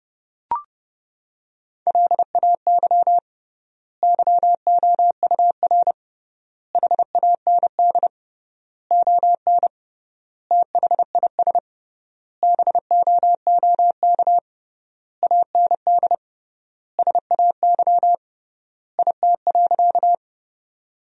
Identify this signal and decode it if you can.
Morse code, heard as a keyed tone, about 30 wpm, 700 Hz. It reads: LAY YOUR HAND ON THIS BOOK AND SAY IT.
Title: 3x Word-Spacing - Sentences from Top 400 Words - 30wpm